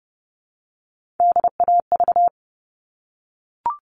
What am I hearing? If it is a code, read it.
Morse code, heard as a keyed tone, about 30 wpm, 700 Hz. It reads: DAV
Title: All Cut Numbers - 3-Digit 30wpm